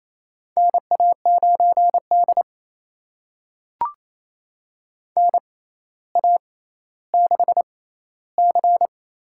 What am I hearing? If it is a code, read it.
Morse code, heard as a keyed tone, about 28 wpm, spaced as for 14 wpm, 700 Hz. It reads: NA9D NA6C
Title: USA Call Signs - 4 Characters - 6x Character Spacing - 28wpm